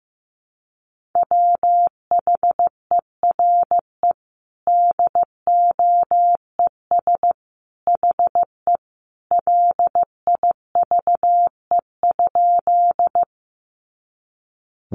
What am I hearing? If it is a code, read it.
Morse code, heard as a keyed tone, about 15 wpm, 700 Hz. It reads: WHERE DOES HE LIVE?
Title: Sentences from Top 200 Words 15wpm